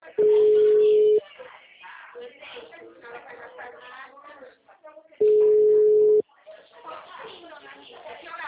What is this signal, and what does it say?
Ringback tone (Europe, 425 Hz), with speech.